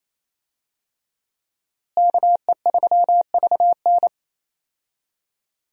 Morse code, heard as a keyed tone, about 27 wpm, 700 Hz. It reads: KE3VN
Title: USA Call Signs - 5 Characters - Repeated 3x - 28wpm